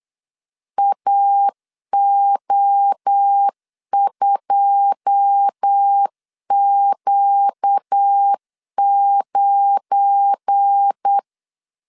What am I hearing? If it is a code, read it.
Morse code, heard as a keyed tone, about 8 wpm, 800 Hz. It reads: AO2Q9
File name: CW / DET / 1 9 1500C S C 8wpm 5653